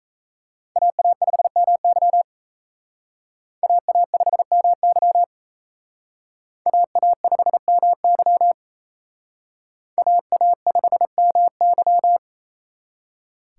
Morse code, heard as a keyed tone, about 35 wpm, 700 Hz. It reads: AA5MY AA5MY AA5MY AA5MY